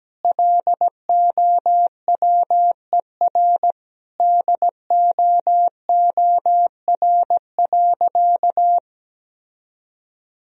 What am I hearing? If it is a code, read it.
Morse code, heard as a keyed tone, about 17 wpm, 700 Hz. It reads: LOWER DOOR.
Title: Sentences from Top 1000 Words - Review Part 4 - 17wpm